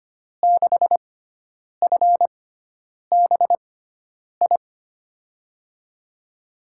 Morse code, heard as a keyed tone, about 25 wpm, spaced as for 9 wpm, 700 Hz. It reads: 6FBI
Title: USA Call Signs - 6x Character Spacing - 25wpm